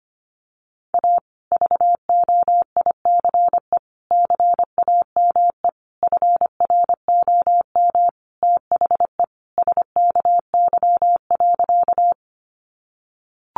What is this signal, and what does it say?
Morse code, heard as a keyed tone, about 25 wpm, 700 Hz. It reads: A VOICE CAME FROM THE SKY.
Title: Sentences from Top 1200 Words 25wpm